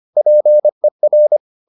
Morse code, heard as a keyed tone, about 25 wpm, 600 Hz. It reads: PER